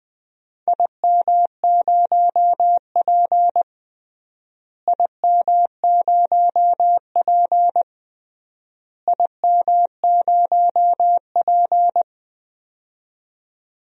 Morse code, heard as a keyed tone, about 20 wpm, 700 Hz. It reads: IM0P IM0P IM0P
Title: DXCC-100 Call Signs - 4 Characters - Repeated 3x - 20wpm